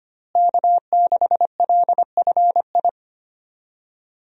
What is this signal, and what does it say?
Morse code, heard as a keyed tone, about 25 wpm, 700 Hz. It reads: K6LFI
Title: USA Call Signs - 5 Characters - Repeated 3x - 25wpm